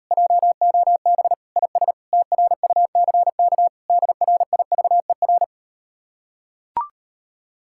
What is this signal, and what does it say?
Morse code, heard as a keyed tone, about 38 wpm, 700 Hz. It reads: JOB IS TRUCK DRIVER